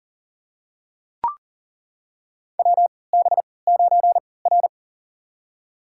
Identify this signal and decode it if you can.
Morse code, heard as a keyed tone, about 40 wpm, spaced as for 23 wpm, 700 Hz. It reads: WB9R
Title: USA Call Signs - 3x Character Spacing - 40wpm